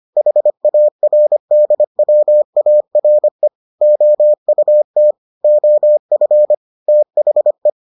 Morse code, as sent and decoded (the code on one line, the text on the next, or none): .... .- .-. -.. .-- .- .-. / --- ..- - / --- ..-. / - .... .
HARDWARE OUT OF THE